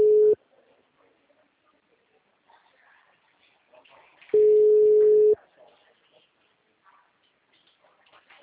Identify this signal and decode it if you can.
Ringback tone (Europe, 425 Hz).
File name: out- -2016- - - .22574